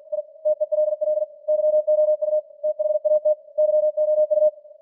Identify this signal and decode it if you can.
Morse code, heard as a keyed tone, about 45 wpm, 610 Hz. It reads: I TELL YOU THAT YOU